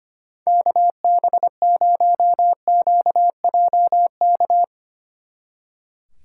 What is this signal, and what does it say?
Morse code, heard as a keyed tone, about 32 wpm, 700 Hz. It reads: KB0QJK